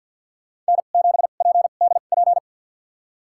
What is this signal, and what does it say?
Morse code, heard as a keyed tone, about 51 wpm, spaced as for 36 wpm, 700 Hz. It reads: N7PDP
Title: USA Call Signs - 5 Characters - 2x Character Spacing - 50wpm